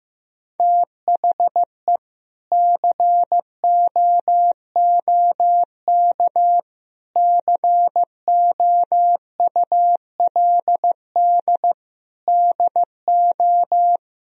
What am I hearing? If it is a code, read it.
Morse code, heard as a keyed tone, about 15 wpm, 700 Hz. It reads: THE COOK COULD DO
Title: Sentences from Top 1000 Words - Review Part 4 - 15wpm